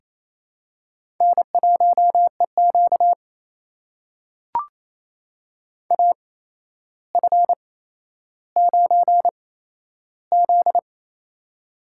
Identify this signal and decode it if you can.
Morse code, heard as a keyed tone, about 28 wpm, spaced as for 12 wpm, 700 Hz. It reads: N1EQ AF9Z
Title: USA Call Signs - 4 Characters - 8x Character Spacing - 28wpm